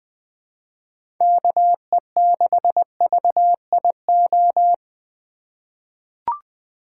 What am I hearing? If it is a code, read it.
Morse code, heard as a keyed tone, about 20 wpm, 700 Hz. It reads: KE6VIO
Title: USA Call Signs - 6 Characters - 3x Character Spacing - 20wpm